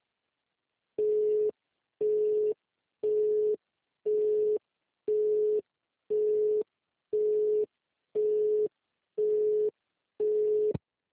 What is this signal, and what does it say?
Busy tone (Europe, 425 Hz).
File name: out- -2002- - - .506